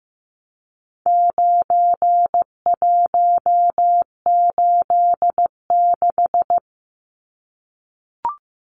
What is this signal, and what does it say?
Morse code, heard as a keyed tone, about 15 wpm, 700 Hz. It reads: 9186
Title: Numbers - 4-Digit 15wpm